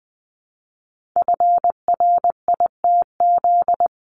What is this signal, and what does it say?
Morse code, heard as a keyed tone, about 20 wpm, 700 Hz. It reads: FRITZ
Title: CWT Exchange 20wpm